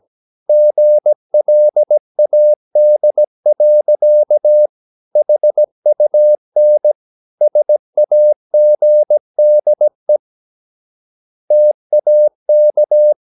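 Morse code, heard as a keyed tone, about 17 wpm, 600 Hz. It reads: GLAD. HUN SAGDE TAK
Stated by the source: Slow (QRS) Morse practice at 17 wpm, from CW med Gnister.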